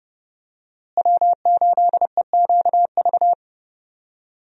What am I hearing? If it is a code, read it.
Morse code, heard as a keyed tone, about 30 wpm, 700 Hz. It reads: W8EQV
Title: USA Call Signs - 5 Characters - Repeated 3x - 30wpm